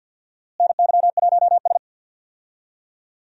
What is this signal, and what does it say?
Morse code, heard as a keyed tone, about 51 wpm, 700 Hz. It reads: NY1S
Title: USA Call Signs - 4 Characters - Repeated 3x - 50wpm